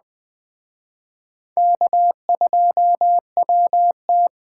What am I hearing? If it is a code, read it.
Morse code, heard as a keyed tone, about 21 wpm, 700 Hz. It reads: K2WT
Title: Speed Racing - USA 1x2 Call Signs 17wpm